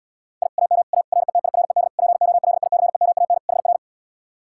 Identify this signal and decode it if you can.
Morse code, heard as a keyed tone, about 77 wpm, 700 Hz. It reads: I AM A RETIRED POLICEMAN HR